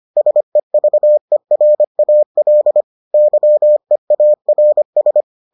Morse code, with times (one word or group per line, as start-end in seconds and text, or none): 0.2-2.8 SEVERAL
3.1-5.2 YEARS